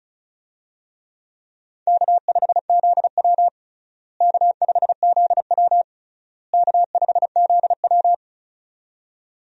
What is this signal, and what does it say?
Morse code, heard as a keyed tone, about 34 wpm, 700 Hz. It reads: K5ZW K5ZW K5ZW
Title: USA Call Signs - 4 Characters - Repeated 3x - 35wpm